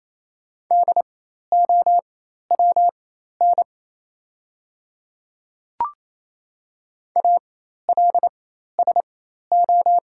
Morse code, heard as a keyed tone, about 27 wpm, spaced as for 13 wpm, 690 Hz. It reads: DOWN ALSO
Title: 4x Character Spacing - Top 200 Words - 28wpm